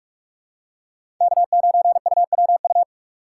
Morse code, heard as a keyed tone, about 44 wpm, 700 Hz. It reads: K9UWU
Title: USA Call Signs - 5 Characters - 6x Character Spacing - 45wpm